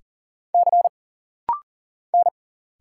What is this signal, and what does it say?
Morse code, heard as a keyed tone, about 40 wpm, 700 Hz. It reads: C N
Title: Single Letter-Number Alt - Rapid-Fire 40wpm